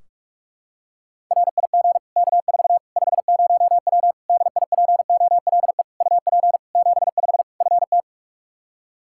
Morse code, heard as a keyed tone, about 46 wpm, 700 Hz. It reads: RIG K4 50W DIPOLE UP 75 FT
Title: Practice QSO - Exchange 2 - Info - 45wpm